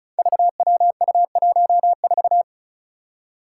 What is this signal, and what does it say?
Morse code, heard as a keyed tone, about 36 wpm, 700 Hz. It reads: VWU14
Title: Five Challenging Letter-Number Combinations 35wpm